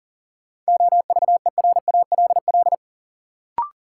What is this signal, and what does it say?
Morse code, heard as a keyed tone, about 40 wpm, 700 Hz. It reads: OVERALL
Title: Sentences from Top 1000 Words - Review Part 7 - New Word Drill - 40wpm